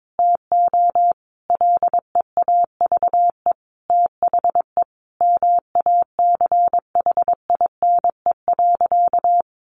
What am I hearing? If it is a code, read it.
Morse code, heard as a keyed tone, about 22 wpm, 700 Hz. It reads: TO LEAVE THE MACHINE.
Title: A Sound Of Thunder 22wpm